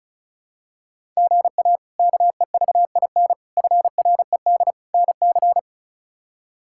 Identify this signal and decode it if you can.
Morse code, heard as a keyed tone, about 36 wpm, 700 Hz. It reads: GA KEVIN FRED NC
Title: K1USN SST QSO Practice - 35wpm